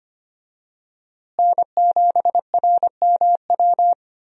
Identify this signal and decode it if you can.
Morse code, heard as a keyed tone, about 25 wpm, 700 Hz. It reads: N7RMW
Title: USA Call Signs - 5 Characters - Repeated 3x - 25wpm